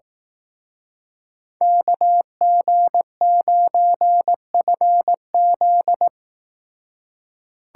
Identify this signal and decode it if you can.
Morse code, heard as a keyed tone, about 18 wpm, 700 Hz. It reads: KG9FZ
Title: Speed Racing - USA 2x2 Call Signs 15wpm